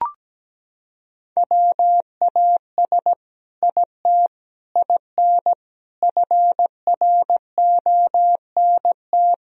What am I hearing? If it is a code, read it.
Morse code, heard as a keyed tone, about 17 wpm, 700 Hz. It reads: WAS IT IN FRONT
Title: Sentences from Top 1000 Words - Review Part 2 - 17wpm